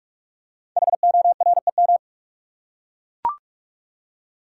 Morse code, heard as a keyed tone, about 44 wpm, 700 Hz. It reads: HOREM
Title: Code Groups - 5 Letters - 45wpm